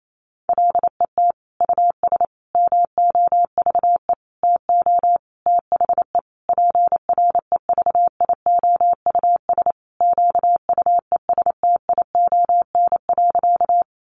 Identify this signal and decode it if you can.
Morse code, heard as a keyed tone, about 28 wpm, 700 Hz. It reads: LET US MOVE TO THE PREVIOUS QUESTION.
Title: Sentences from Top 1500 Words 28wpm